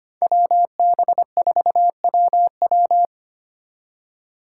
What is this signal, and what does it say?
Morse code, heard as a keyed tone, about 33 wpm, 700 Hz. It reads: WB4WW